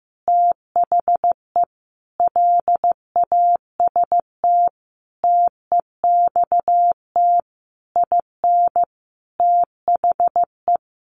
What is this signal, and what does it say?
Morse code, heard as a keyed tone, about 14 wpm, 700 Hz. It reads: THE LAST TEXT IN THE
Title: Sentences from Top 1200 Words 15wpm